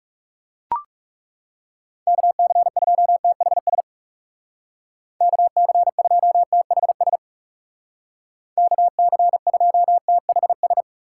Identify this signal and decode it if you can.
Morse code, heard as a keyed tone, about 38 wpm, 700 Hz. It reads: KC2THS KC2THS KC2THS